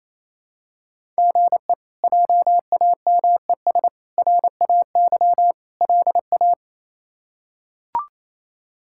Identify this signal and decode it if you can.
Morse code, heard as a keyed tone, about 28 wpm, 700 Hz. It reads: GE JAMES RAY LA
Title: K1USN SST QSO Practice - 28wpm